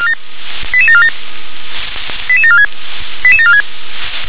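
Five-tone selective call (ZVEI), heard as quick runs of five tones, tones of about 70 ms. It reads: 88747 88746 88746
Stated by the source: ZVEI five-tone call 88759 received by SDR.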